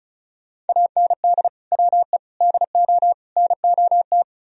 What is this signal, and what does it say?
Morse code, heard as a keyed tone, about 45 wpm, 700 Hz. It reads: AND WE DO NOT